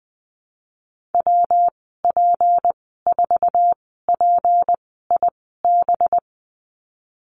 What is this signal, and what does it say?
Morse code, heard as a keyed tone, about 20 wpm, 700 Hz. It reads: W P 4 P I B
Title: 231 Lesson 37g - Letter x - Call Signs 20wpm